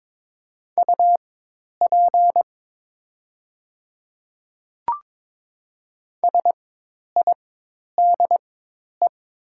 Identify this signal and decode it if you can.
Morse code, heard as a keyed tone, about 22 wpm, spaced as for 10 wpm, 700 Hz. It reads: UP SIDE
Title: 4x Character Spacing - Top 100 Words - 22wpm